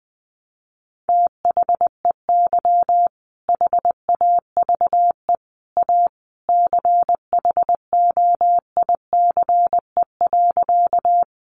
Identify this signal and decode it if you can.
Morse code, heard as a keyed tone, about 20 wpm, 700 Hz. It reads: THEY HAVE A CHOICE.
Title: Sentences from Top 900 Words 20wpm